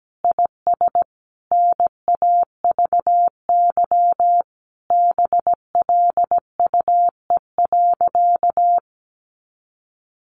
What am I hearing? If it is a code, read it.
Morse code, heard as a keyed tone, about 17 wpm, 700 Hz. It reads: IS NAVY BLUE.
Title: Additional Easy Phrases 17wpm